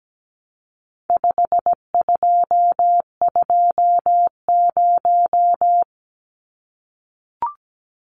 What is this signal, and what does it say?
Morse code, heard as a keyed tone, about 17 wpm, 700 Hz. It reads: 5220
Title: Numbers - 4-Digit 17wpm